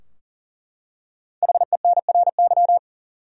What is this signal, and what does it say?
Morse code, heard as a keyed tone, about 40 wpm, 700 Hz. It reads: HENRY
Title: USA Common Names - 40wpm